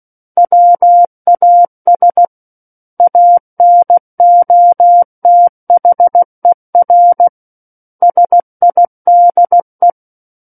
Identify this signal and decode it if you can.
Morse code, heard as a keyed tone, about 16 wpm, 700 Hz. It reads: WAS ANOTHER SIDE
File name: Book One / 07-Chapter 7 - 16 WPM